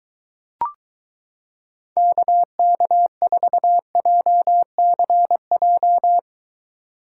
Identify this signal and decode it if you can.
Morse code, heard as a keyed tone, about 23 wpm, 700 Hz. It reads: KK4JCJ